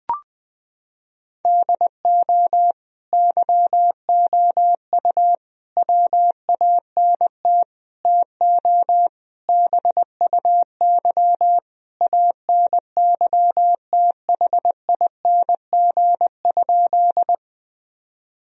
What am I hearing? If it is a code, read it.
Morse code, heard as a keyed tone, about 20 wpm, 700 Hz. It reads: DO YOU WANT TO BUY ANYTHING?
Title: Sentences from Top 1000 Words - Review Part 4 - 20wpm